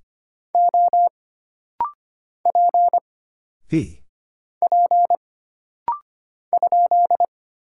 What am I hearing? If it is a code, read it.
Morse code, heard as a keyed tone, about 25 wpm, 700 Hz. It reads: O P P ?
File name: Single Character (Letter:Number:Punctuation:Prosign) - Rapid-Fire - 25wpm